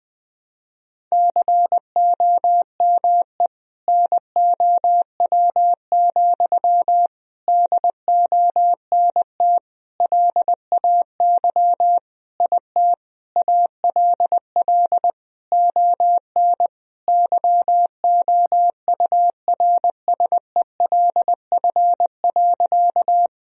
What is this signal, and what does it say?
Morse code, heard as a keyed tone, about 21 wpm, 700 Hz. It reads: COME NOW, DONT LAY IT ALL ON YOURSELF.
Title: Sentences from Top 1000 Words - Review Part 2 - 20wpm